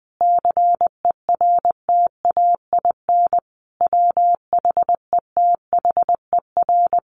Morse code, heard as a keyed tone, about 20 wpm, 700 Hz. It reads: CERTAIN WHETHER
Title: A Sound Of Thunder 20wpm